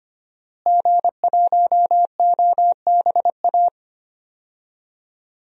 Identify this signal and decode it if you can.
Morse code, heard as a keyed tone, about 25 wpm, 700 Hz. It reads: G1OBA
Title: DXCC-100 Call Signs - 5 Characters - Repeated 3x - 25wpm